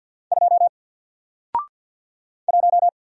Morse code, heard as a keyed tone, about 50 wpm, 700 Hz. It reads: 2 1
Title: Single Character (Letter:Number:Punctuation:Prosign) 50wpm